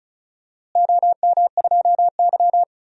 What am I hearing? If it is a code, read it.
Morse code, heard as a keyed tone, about 36 wpm, 700 Hz. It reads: OM2Y
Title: DXCC-100 Call Signs - 4 Characters - Repeated 2x - 35wpm